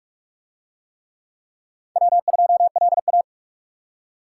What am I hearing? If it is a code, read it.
Morse code, heard as a keyed tone, about 46 wpm, 700 Hz. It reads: W2LA